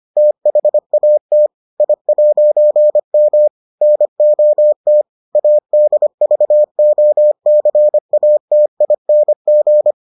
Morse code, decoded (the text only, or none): THAT I'M NOT ADVOCATING